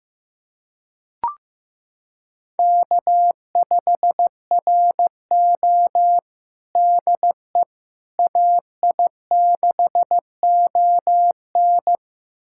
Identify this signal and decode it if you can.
Morse code, heard as a keyed tone, about 15 wpm, 700 Hz. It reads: K5RO DE AI6ON
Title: USA Callsign DE Callsign 15wpm